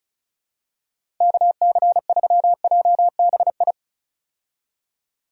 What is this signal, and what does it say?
Morse code, heard as a keyed tone, about 36 wpm, 700 Hz. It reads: KC3JBI
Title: USA Call Signs - 6 Characters - 35wpm